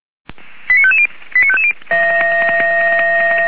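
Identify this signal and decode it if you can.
Five-tone selective call (ZVEI), heard as quick runs of five tones, tones of about 70 ms. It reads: 89550 89550
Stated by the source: ZVEI five-tone call 89136 received by SDR, recorded by feuerwehr-orlamuende.org fire brigade.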